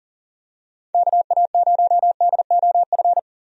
Morse code, heard as a keyed tone, about 38 wpm, 700 Hz. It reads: KA0DOF
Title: USA Call Signs - 6 Characters - 40wpm